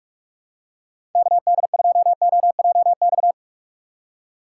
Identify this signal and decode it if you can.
Morse code, heard as a keyed tone, about 46 wpm, 700 Hz. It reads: KD2OJX
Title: USA Call Signs - 6 Characters - 6x Character Spacing - 45wpm